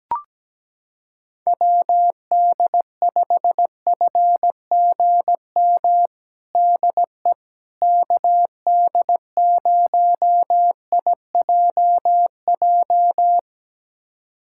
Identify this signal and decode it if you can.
Morse code, heard as a keyed tone, about 17 wpm, 700 Hz. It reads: WD5FGM DE KD0IJJ
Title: USA Callsign DE Callsign - 6 Characters - 17wpm